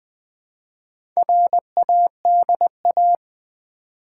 Morse code, heard as a keyed tone, about 20 wpm, 700 Hz. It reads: RADA
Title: Build A Word - 1 Letter At A Time - 5-Letter Words - 20wpm